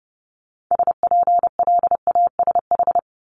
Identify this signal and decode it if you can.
Morse code, heard as a keyed tone, about 30 wpm, 700 Hz. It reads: SPLASH
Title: Multi-letter Phonemes 30wpm